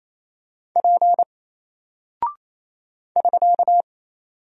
Morse code, heard as a keyed tone, about 27 wpm, 700 Hz. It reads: P <SK>
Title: Single Character (Letter:Number:Punctuation:Prosign) Alt 28wpm